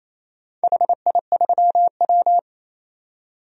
Morse code, heard as a keyed tone, about 28 wpm, 700 Hz. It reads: HI3W
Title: DXCC-100 Call Signs - 4 Characters - 28wpm